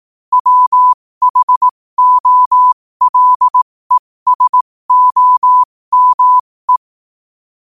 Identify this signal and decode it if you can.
Morse code, heard as a keyed tone, about 18 wpm, spaced as for 15 wpm, 1000 Hz. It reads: WHOLESOME
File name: qotdfiles / 15wpm